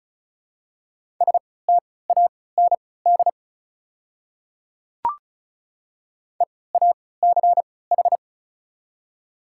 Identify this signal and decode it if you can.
Morse code, heard as a keyed tone, about 36 wpm, spaced as for 20 wpm, 700 Hz. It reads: STAND EACH
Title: 3x Character Spacing - Top 300 Words - 35wpm